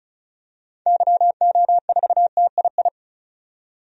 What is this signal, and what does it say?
Morse code, heard as a keyed tone, about 34 wpm, 700 Hz. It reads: YO4TII
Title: DXCC-100 Call Signs - 6 Characters - 35wpm